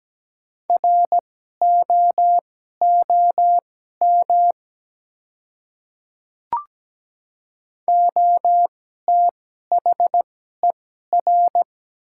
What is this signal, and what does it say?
Morse code, heard as a keyed tone, about 17 wpm, spaced as for 12 wpm, 700 Hz. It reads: ROOM OTHER
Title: 2x Character Spacing - Top 300 Words - 17wpm